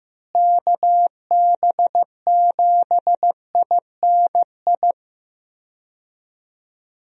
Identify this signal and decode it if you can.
Morse code, heard as a keyed tone, about 16 wpm, 700 Hz. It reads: KB7INI
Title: USA Call Signs - 6 Characters - Repeated 2x - 15wpm